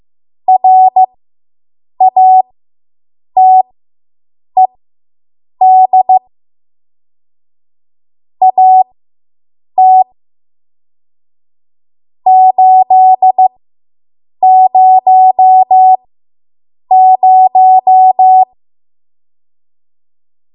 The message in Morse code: RATED AT 800